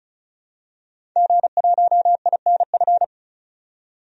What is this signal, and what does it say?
Morse code, heard as a keyed tone, about 36 wpm, 700 Hz. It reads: G1INF
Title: DXCC-100 Call Signs - 5 Characters - 35wpm